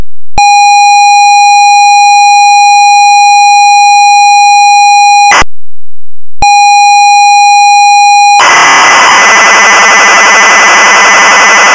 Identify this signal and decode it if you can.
ZX Spectrum tape loading.
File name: ZX Spectrum - Loading Sound